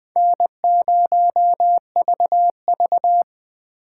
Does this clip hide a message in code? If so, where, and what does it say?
Yes: Morse code from 0.2 to 3.2 s, heard as a keyed tone, about 21 wpm, 700 Hz. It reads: N0VV